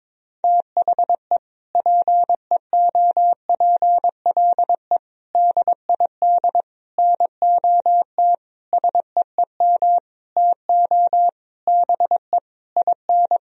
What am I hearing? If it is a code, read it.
Morse code, heard as a keyed tone, about 21 wpm, 700 Hz. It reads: THE PEOPLE DID NOT SEEM TO BE IN
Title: Sentences from Top 1000 Words - Review Part 1 - 22wpm